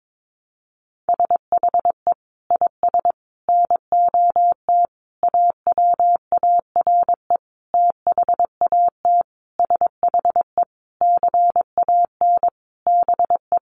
Morse code, heard as a keyed tone, about 22 wpm, 700 Hz. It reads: SHE IS NOT AWARE THAT SHE CAN BE
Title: Sentences from Top 1200 Words 22wpm